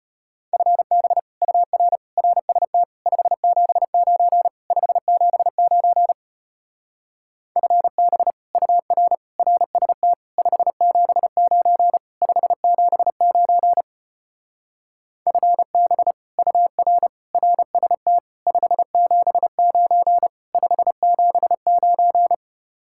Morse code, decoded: FB UR RST 579 579 FB UR RST 579 579 FB UR RST 579 579

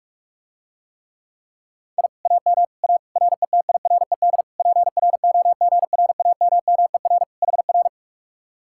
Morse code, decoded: I AM A RETIRED PROGRAMMER HR

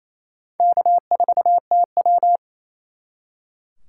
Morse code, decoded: K4TW

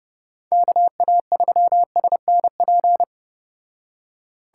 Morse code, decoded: KA3SNP

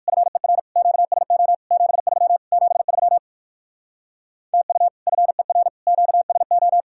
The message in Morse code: FER QSO 73 73 TU FER QSO

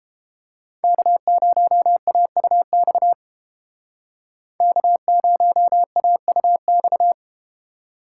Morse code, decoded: K0AUX K0AUX

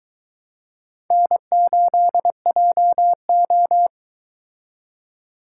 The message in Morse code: N8JO

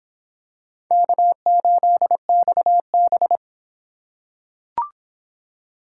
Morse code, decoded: K8XB

